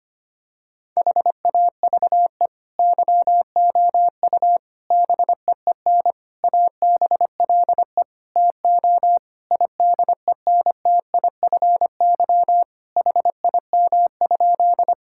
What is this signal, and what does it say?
Morse code, heard as a keyed tone, about 25 wpm, 700 Hz. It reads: HAVE YOU BEEN ABLE TO IDENTIFY HIM?
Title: Sentences from Top 1000 Words - Review Part 6 - 25wpm